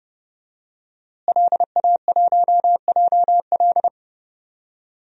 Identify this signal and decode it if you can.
Morse code, heard as a keyed tone, about 29 wpm, 700 Hz. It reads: LA1JL
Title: DXCC-100 Call Signs - 5 Characters - Repeated 2x - 30wpm